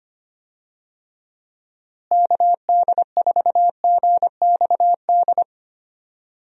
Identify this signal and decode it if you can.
Morse code, heard as a keyed tone, about 25 wpm, 700 Hz. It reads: KD4GXD
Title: USA Call Signs - 6 Characters - Repeated 2x - 25wpm